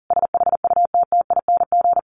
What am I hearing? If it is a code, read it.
Morse code, heard as a keyed tone, about 40 wpm, 700 Hz. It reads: SHUTTING
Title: A Sound Of Thunder 40wpm